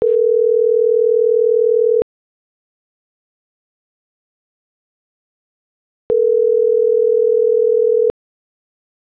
Ringback tone (North America), 440 + 480 Hz.
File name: CENTRAL SOTOVA / out roberto.miranda - .43413